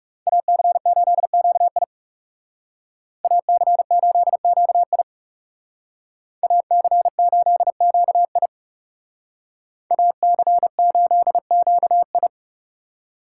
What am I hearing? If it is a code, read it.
Morse code, heard as a keyed tone, about 39 wpm, 700 Hz. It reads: AC8QI AC8QI AC8QI AC8QI